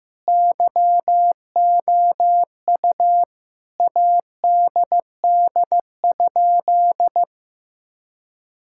Morse code, heard as a keyed tone, about 15 wpm, 700 Hz. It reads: YOU ADD?
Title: Sentences from Top 1000 Words - Review Part 5 - 15wpm